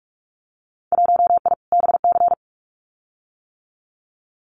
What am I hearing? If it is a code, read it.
Morse code, heard as a keyed tone, about 44 wpm, 700 Hz. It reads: 1I BC
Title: Winter Field Day Exchanges Encore 45wpm